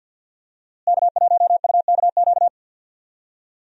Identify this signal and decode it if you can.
Morse code, heard as a keyed tone, about 51 wpm, 700 Hz. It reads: K1UKQ
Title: USA Call Signs - 5 Characters - Repeated 2x - 50wpm